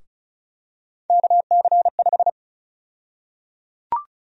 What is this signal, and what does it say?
Morse code, heard as a keyed tone, about 34 wpm, 700 Hz. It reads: KC5
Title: USA Call Sign Prefixes - 35wpm